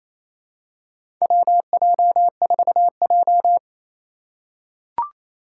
Morse code, heard as a keyed tone, about 28 wpm, 700 Hz. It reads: WJ4J
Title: USA Call Signs - 4 Characters - 4x Character Spacing - 28wpm